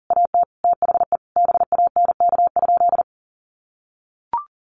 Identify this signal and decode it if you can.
Morse code, heard as a keyed tone, about 40 wpm, 700 Hz. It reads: AT THE BANK?
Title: Sentences from Top 1300 Words 40wpm